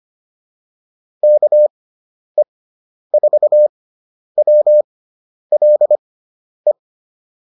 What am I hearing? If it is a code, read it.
Morse code, heard as a keyed tone, about 25 wpm, spaced as for 10 wpm, 600 Hz. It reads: KE4WLE